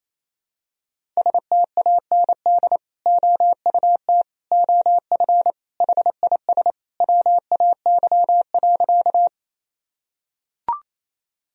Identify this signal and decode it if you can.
Morse code, heard as a keyed tone, about 28 wpm, 700 Hz. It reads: STAND OUT OF HIS WAY.
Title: Sentences from Top 1000 Words - Review Part 1 - 28wpm